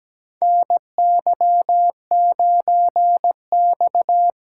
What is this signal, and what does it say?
Morse code, heard as a keyed tone, about 17 wpm, 700 Hz. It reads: NY9X
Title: K1USN SST QSO Practice - 17wpm